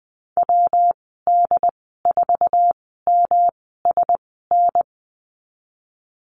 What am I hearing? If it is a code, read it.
Morse code, heard as a keyed tone, about 20 wpm, spaced as for 15 wpm, 700 Hz. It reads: WD4MSN
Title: 147 Lesson 24g - Letter y - Call Signs 20wpm